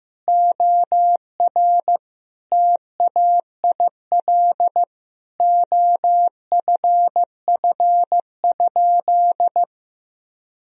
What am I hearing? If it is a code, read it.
Morse code, heard as a keyed tone, about 15 wpm, 700 Hz. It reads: OR TAIL OFF?
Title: Sentences from Top 1000 Words - Review Part 3 - 15wpm